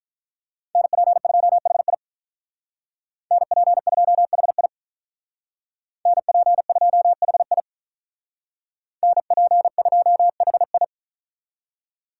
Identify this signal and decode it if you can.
Morse code, heard as a keyed tone, about 44 wpm, 700 Hz. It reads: NP2HI NP2HI NP2HI NP2HI
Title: Speed Racing - USA 2x2 Call Signs 35wpm